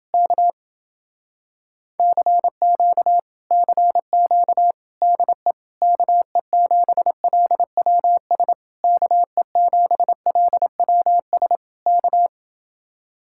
Morse code, heard as a keyed tone, about 29 wpm, 700 Hz. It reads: K CQ CQ DE KE7LWS KE7LWS K